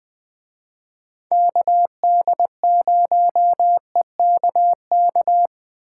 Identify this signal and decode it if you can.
Morse code, heard as a keyed tone, about 21 wpm, 700 Hz. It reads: KD0EKK